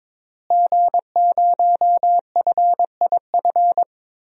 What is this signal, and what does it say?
Morse code, heard as a keyed tone, about 22 wpm, 700 Hz. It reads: G0FIF
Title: DXCC-100 Call Signs - 5 Characters - Repeated 3x - 22wpm